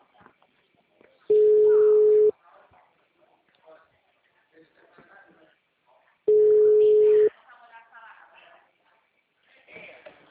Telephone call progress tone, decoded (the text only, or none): ringback tone